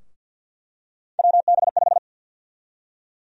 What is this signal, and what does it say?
Morse code, heard as a keyed tone, about 48 wpm, 700 Hz. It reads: WB5
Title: USA Call Sign Prefixes - Repeated 2x - 50wpm